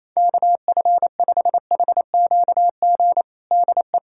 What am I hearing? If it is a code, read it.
Morse code, heard as a keyed tone, about 28 wpm, 700 Hz. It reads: KF5HQG DE
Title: USA Callsign DE Callsign 28wpm